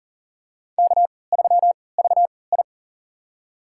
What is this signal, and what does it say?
Morse code, heard as a keyed tone, about 40 wpm, spaced as for 23 wpm, 700 Hz. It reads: K3VI
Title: USA Call Signs - 4 Characters - 3x Character Spacing - 40wpm